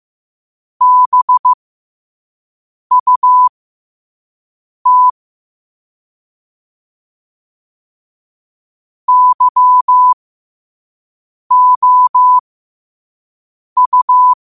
Morse code, heard as a keyed tone, about 15 wpm, spaced as for 5 wpm, 1000 Hz. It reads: BUT YOU
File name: qotdfiles / 5wpm